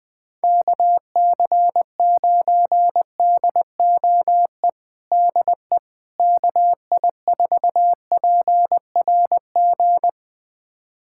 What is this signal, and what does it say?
Morse code, heard as a keyed tone, about 20 wpm, 700 Hz. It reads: KC9DOE DE KI4PRG
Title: USA Callsign DE Callsign - 6 Characters - 20wpm